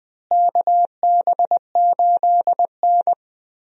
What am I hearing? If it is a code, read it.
Morse code, heard as a keyed tone, about 20 wpm, 700 Hz. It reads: KB8N
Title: USA Call Signs - 4 Characters - Repeated 3x - 20wpm